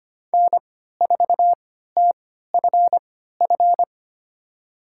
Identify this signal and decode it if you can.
Morse code, heard as a keyed tone, about 25 wpm, spaced as for 14 wpm, 700 Hz. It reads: N4TFF
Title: USA Call Signs - 5 Characters - 3x Character Spacing - 25wpm